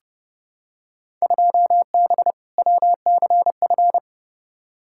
Morse code, heard as a keyed tone, about 30 wpm, 700 Hz. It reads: 2B WCF
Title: Field Day Exchanges Encore - 30wpm